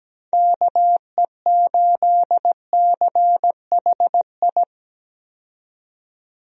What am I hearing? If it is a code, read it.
Morse code, heard as a keyed tone, about 17 wpm, 700 Hz. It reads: KE8CHI
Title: USA Call Signs - 6 Characters - 17wpm